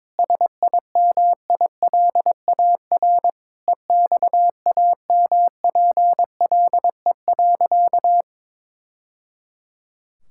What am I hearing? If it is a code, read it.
Morse code, heard as a keyed tone, about 22 wpm, 700 Hz. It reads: SIMILAR EXAMPLE.
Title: Sentences from Top 1000 Words - Review Part 5 - 22wpm